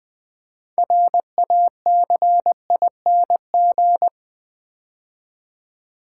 Morse code, heard as a keyed tone, about 20 wpm, 700 Hz. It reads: RACING
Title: 6 Letter Words - First 3 Letters Given - 20wpm